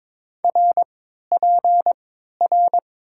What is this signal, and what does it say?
Morse code, heard as a keyed tone, about 21 wpm, spaced as for 12 wpm, 700 Hz. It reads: RPR
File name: Top 90 QSO Elements - 3x Character Spacing - 22wpm